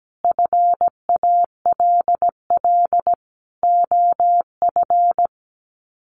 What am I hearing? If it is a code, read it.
Morse code, heard as a keyed tone, about 17 wpm, 700 Hz. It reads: FALL OF
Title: Sets of 5 Words Encore2 17wpm